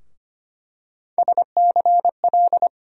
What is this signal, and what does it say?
Morse code, heard as a keyed tone, about 25 wpm, 700 Hz. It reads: SCL
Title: USA Call Sign Postfixes - 3 Letters 25wpm